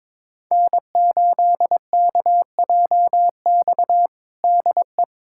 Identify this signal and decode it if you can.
Morse code, heard as a keyed tone, about 22 wpm, 700 Hz. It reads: N8KJX DE
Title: USA Callsign DE Callsign - 5 Characters - 22wpm